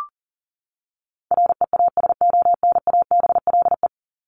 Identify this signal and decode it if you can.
Morse code, heard as a keyed tone, about 40 wpm, 700 Hz. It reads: REASONABLE
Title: Top 1501-1750 Words 40wpm